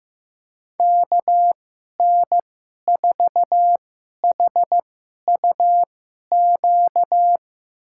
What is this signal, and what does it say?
Morse code, heard as a keyed tone, about 15 wpm, spaced as for 11 wpm, 700 Hz. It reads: KN4HUQ
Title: USA Call Signs - 2x Character Spacing - 15wpm